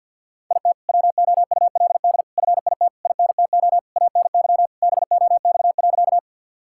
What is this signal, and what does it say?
Morse code, heard as a keyed tone, about 50 wpm, 690 Hz. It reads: IT WOULD FIT INTO ANY BOX.
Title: Sentences from Top 1000 Words - Review Part 5 - 50wpm